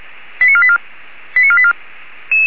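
Five-tone selective call (ZVEI), heard as quick runs of five tones, tones of about 70 ms. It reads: 89484 89484
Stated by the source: ZVEI five-tone selective call received on SDR by the Orlamünde fire brigade (feuerwehr-orlamuende).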